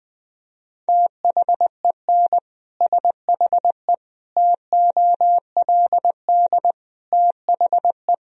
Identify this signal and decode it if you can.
Morse code, heard as a keyed tone, about 20 wpm, 700 Hz. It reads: THEN SHE TOLD THE
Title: Sentences from Top 1000 Words - Review Part 2 - 20wpm